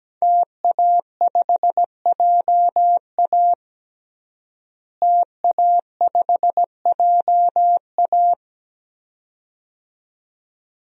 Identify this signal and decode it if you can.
Morse code, heard as a keyed tone, about 17 wpm, 700 Hz. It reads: TA5JA TA5JA
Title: DXCC-100 Call Signs - 5 Characters - Repeated 2x - 17wpm